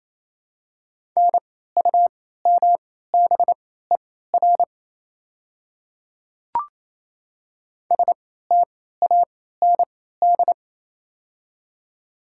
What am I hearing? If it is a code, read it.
Morse code, heard as a keyed tone, about 28 wpm, spaced as for 16 wpm, 700 Hz. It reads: NUMBER STAND
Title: 3x Character Spacing - Top 300 Words - 28wpm